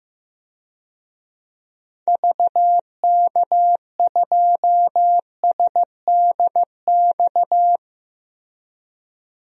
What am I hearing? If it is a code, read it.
Morse code, heard as a keyed tone, about 15 wpm, 700 Hz. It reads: VK2SDX
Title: DXCC-100 Call Signs - 6 Characters - Repeated 3x - 15wpm